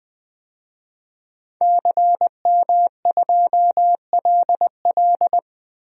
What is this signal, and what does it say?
Morse code, heard as a keyed tone, about 20 wpm, 700 Hz. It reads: CM2LL
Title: DXCC-100 Call Signs - 5 Characters - Repeated 2x - 20wpm